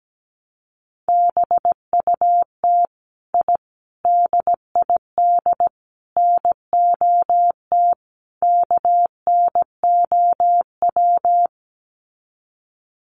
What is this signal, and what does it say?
Morse code, heard as a keyed tone, about 17 wpm, 700 Hz. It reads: BUT I DID NOT KNOW
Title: Sets of 5 Words Encore 17wpm